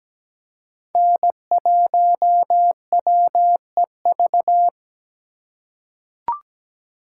Morse code, heard as a keyed tone, about 17 wpm, 700 Hz. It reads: N1WEV
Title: USA Call Signs - 5 Characters - 4x Character Spacing - 17wpm